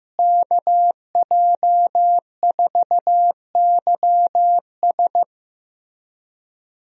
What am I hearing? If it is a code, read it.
Morse code, heard as a keyed tone, about 14 wpm, 700 Hz. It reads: KJ4YS
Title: USA Call Signs - 5 Characters - Repeated 3x - 15wpm